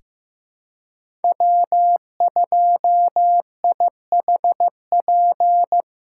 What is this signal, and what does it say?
Morse code, heard as a keyed tone, about 15 wpm, 700 Hz. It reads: W2IHP
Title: USA Call Signs - 2x Character Spacing - 15wpm